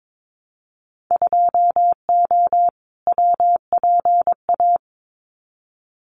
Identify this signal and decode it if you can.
Morse code, heard as a keyed tone, about 22 wpm, 700 Hz. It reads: 2O WPA
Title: Winter Field Day Exchanges Encore 22wpm